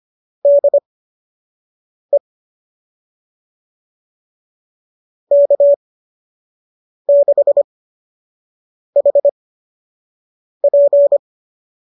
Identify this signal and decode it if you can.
Morse code, heard as a keyed tone, about 25 wpm, spaced as for 6 wpm, 570 Hz. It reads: DE K6HP
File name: qso003 06